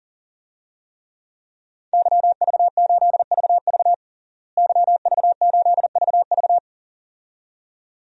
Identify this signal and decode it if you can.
Morse code, heard as a keyed tone, about 40 wpm, 700 Hz. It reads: YV8VV YV8VV
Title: DXCC-100 Call Signs - 5 Characters - Repeated 2x - 40wpm